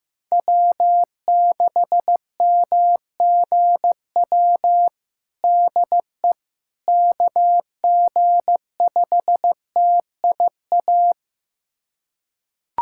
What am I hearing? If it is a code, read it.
Morse code, heard as a keyed tone, about 15 wpm, 700 Hz. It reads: W6MGW DE KG5TIA
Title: USA Callsign DE Callsign 15wpm